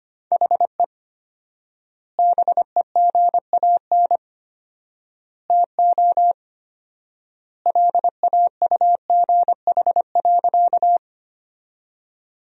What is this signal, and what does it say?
Morse code, heard as a keyed tone, about 25 wpm, 700 Hz. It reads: HE BEGAN TO LAUGH.
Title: 4x Word-Spacing - Sentences from Top 500 Words - 25wpm